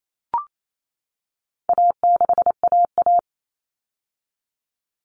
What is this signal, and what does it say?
Morse code, heard as a keyed tone, about 28 wpm, 700 Hz. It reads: A6AA
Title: All Cut Numbers - 4-Digit 28wpm